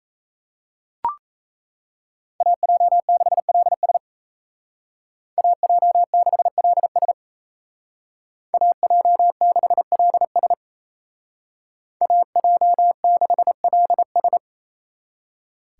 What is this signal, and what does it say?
Morse code, heard as a keyed tone, about 33 wpm, 700 Hz. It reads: AJ6LS AJ6LS AJ6LS AJ6LS
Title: Speed Racing - USA 2x2 Call Signs 28wpm